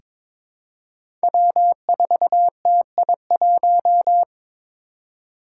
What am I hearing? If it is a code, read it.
Morse code, heard as a keyed tone, about 22 wpm, 700 Hz. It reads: W4TI1
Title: Five Challenging Letter-Number Combinations 22wpm